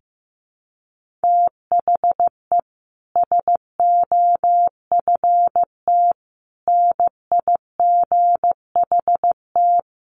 Morse code, heard as a keyed tone, about 15 wpm, 700 Hz. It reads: THE SOFT NIGHT
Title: Sentences from Top 1500 Words 15wpm